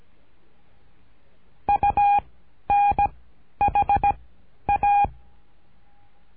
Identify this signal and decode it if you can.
Morse code, heard as a keyed tone, about 17 wpm, 810 Hz. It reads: U N H A
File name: cursos / CURSO DE TELEGRAFIA / Telegrafia - Trilha 13